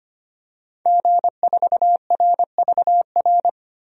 Morse code, heard as a keyed tone, about 25 wpm, 700 Hz. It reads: G4RVR
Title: DXCC-100 Call Signs - 5 Characters - 25wpm